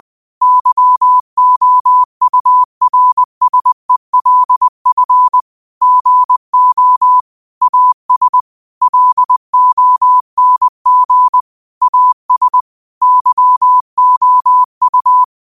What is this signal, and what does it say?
Morse code, heard as a keyed tone, about 20 wpm, 1000 Hz. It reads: YOURSELF GO AS LONG AS YOU
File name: qotdfiles / 20wpm